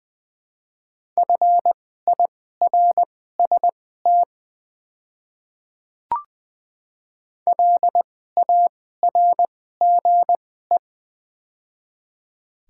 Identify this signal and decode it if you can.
Morse code, heard as a keyed tone, about 21 wpm, spaced as for 15 wpm, 700 Hz. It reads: FIRST LARGE
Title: 2x Character Spacing - Top 200 Words - 20wpm